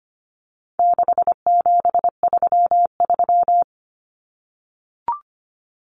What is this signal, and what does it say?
Morse code, heard as a keyed tone, about 25 wpm, 700 Hz. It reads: 6733
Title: Numbers - 4-Digit 25wpm